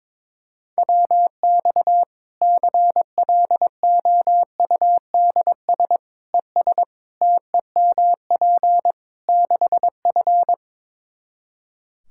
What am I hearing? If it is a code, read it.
Morse code, heard as a keyed tone, about 22 wpm, 700 Hz. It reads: WX CLOUDS ES TEMP 6F
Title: Practice QSO - Exchange 2 - Info - 22wpm